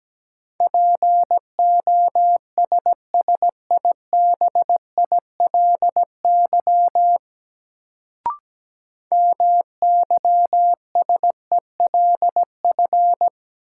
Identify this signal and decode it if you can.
Morse code, heard as a keyed tone, about 17 wpm, 700 Hz. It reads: POSSIBILY MYSELF
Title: Sentences from Top 1300 Words - New Word Drill - 17wpm